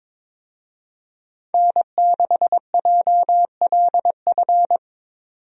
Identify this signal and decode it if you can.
Morse code, heard as a keyed tone, about 22 wpm, 700 Hz. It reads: N6JLF
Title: USA Call Signs - 5 Characters - 22wpm